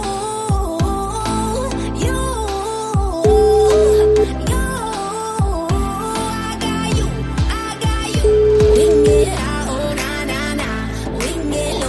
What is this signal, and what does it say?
Ringback tone (Europe, 425 Hz), with noise.